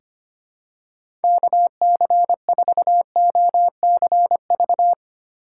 Morse code, heard as a keyed tone, about 25 wpm, 700 Hz. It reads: KC4OCV
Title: USA Call Signs - 8x Character Spacing - 25wpm